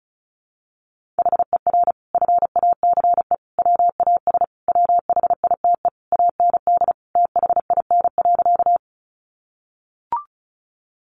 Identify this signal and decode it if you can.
Morse code, heard as a keyed tone, about 34 wpm, 700 Hz. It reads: HER FACE WAS WHITE AND THIN.
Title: Sentences from Top 1200 Words 35wpm